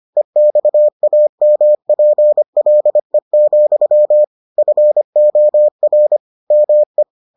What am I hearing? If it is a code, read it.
Morse code, heard as a keyed tone, about 25 wpm, 600 Hz. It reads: EXAMPLE, FOR ME